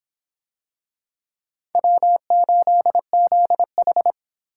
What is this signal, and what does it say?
Morse code, heard as a keyed tone, about 26 wpm, 700 Hz. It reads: W8ZH